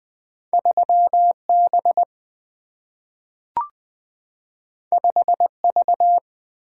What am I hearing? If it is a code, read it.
Morse code, heard as a keyed tone, about 20 wpm, 700 Hz. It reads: 3B 5V
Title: Single-Letter Single-Number 20wpm